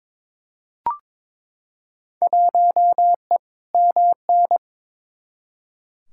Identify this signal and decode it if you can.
Morse code, heard as a keyed tone, about 22 wpm, 700 Hz. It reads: 1E MN
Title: Field Day Exchanges - 22wpm